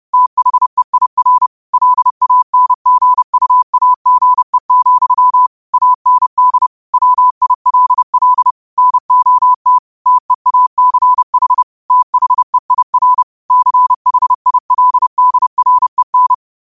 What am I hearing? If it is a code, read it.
Morse code, heard as a keyed tone, about 30 wpm, 1000 Hz. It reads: THEIR LANGUAGE, AND WILL NOT TEACH THEIR CHILDREN
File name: qotdfiles / 30wpm